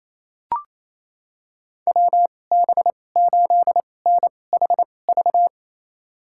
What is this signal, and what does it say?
Morse code, heard as a keyed tone, about 28 wpm, spaced as for 20 wpm, 700 Hz. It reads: WB8NHV
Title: USA Call Signs - 2x Character Spacing - 28wpm